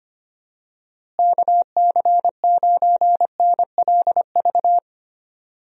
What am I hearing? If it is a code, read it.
Morse code, heard as a keyed tone, about 25 wpm, 700 Hz. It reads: KC9NLV
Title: USA Call Signs - 6 Characters - Repeated 3x - 25wpm